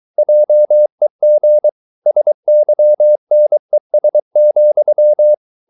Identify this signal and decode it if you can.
Morse code, heard as a keyed tone, about 23 wpm, 600 Hz. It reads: JEG SYNES,
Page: Ferie 23 wpm